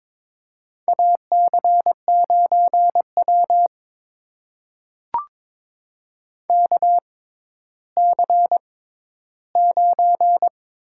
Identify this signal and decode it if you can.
Morse code, heard as a keyed tone, about 22 wpm, spaced as for 12 wpm, 700 Hz. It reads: AC9W KC9
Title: USA Call Signs - 6x Character Spacing - 22wpm